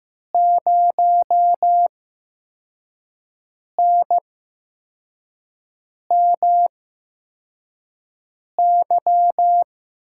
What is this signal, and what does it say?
Morse code, heard as a keyed tone, about 15 wpm, spaced as for 4 wpm, 700 Hz. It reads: 0NMY
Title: USA Call Signs - 8x Character Spacing - 15wpm